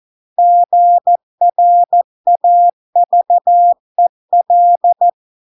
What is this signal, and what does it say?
Morse code, heard as a keyed tone, about 14 wpm, 700 Hz. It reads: GRAVEL